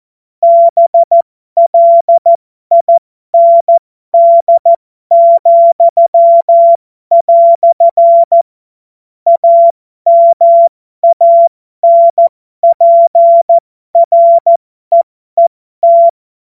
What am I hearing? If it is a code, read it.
Morse code, heard as a keyed tone, about 14 wpm, spaced as for 12 wpm, 680 Hz. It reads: BLIND," AMANPREET